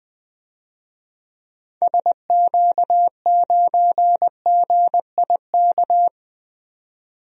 Morse code, heard as a keyed tone, about 21 wpm, 700 Hz. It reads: SQ9GIK
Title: DXCC-100 Call Signs - 6 Characters - Repeated 3x - 20wpm